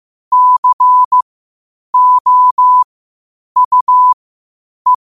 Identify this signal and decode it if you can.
Morse code, heard as a keyed tone, about 15 wpm, spaced as for 8 wpm, 1000 Hz. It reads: COUE
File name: qotdfiles / 8wpm